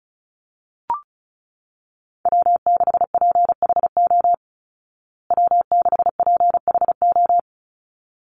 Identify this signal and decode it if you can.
Morse code, heard as a keyed tone, about 34 wpm, 700 Hz. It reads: W6PHO W6PHO
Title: Call Signs Repeated 3x - 35wpm